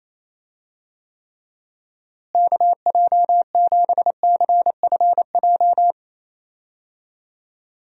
Morse code, heard as a keyed tone, about 28 wpm, 700 Hz. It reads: KJ7CFJ
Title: USA Call Signs - 6 Characters - Repeated 2x - 28wpm